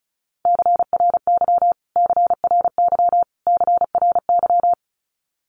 Morse code, heard as a keyed tone, about 34 wpm, 700 Hz. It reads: CRY CRY CRY
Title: Top 500 Words Repeated 35wpm